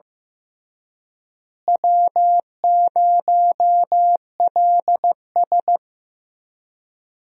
Morse code, heard as a keyed tone, about 15 wpm, 700 Hz. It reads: W0LS
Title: USA Call Signs - 4 Characters - Repeated 2x - 15wpm